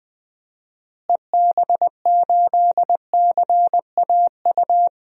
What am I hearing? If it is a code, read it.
Morse code, heard as a keyed tone, about 21 wpm, 700 Hz. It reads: EB8CAU
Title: DXCC-100 Call Signs - 6 Characters - Repeated 3x - 20wpm